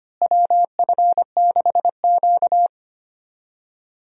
Morse code, heard as a keyed tone, about 25 wpm, 700 Hz. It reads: WF6Q
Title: USA Call Signs - 4 Characters - Repeated 3x - 25wpm